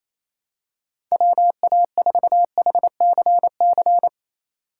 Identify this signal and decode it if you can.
Morse code, heard as a keyed tone, about 27 wpm, 700 Hz. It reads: WA4HCC
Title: K1USN SST QSO Practice - 28wpm